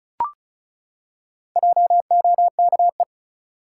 Morse code, heard as a keyed tone, about 36 wpm, 700 Hz. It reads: JOKE
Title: Four Letter Words - 35wpm